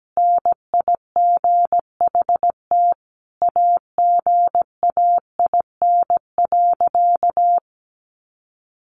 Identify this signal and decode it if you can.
Morse code, heard as a keyed tone, about 17 wpm, 700 Hz. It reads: NIGHT AGAIN.
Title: Sentences from Top 300 Words 17wpm